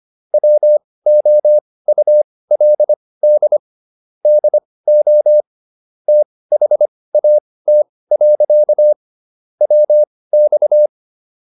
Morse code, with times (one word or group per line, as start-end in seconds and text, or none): 0.3-3.6 WOULD
4.3-5.4 DO
6.1-8.9 THAT.
9.6-10.9 WX